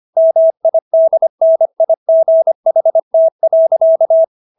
Morse code, heard as a keyed tone, about 25 wpm, 650 Hz. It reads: MIDNIGHT.